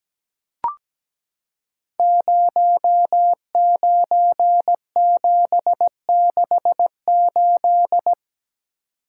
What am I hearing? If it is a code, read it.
Morse code, heard as a keyed tone, about 17 wpm, 700 Hz. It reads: 09768
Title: Code Groups - 5 Letter:Numbers - 17wpm